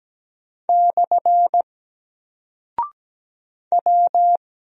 Morse code, heard as a keyed tone, about 17 wpm, 700 Hz. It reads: / W
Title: Single Character (Letter:Number:Punctuation:Prosign) 17wpm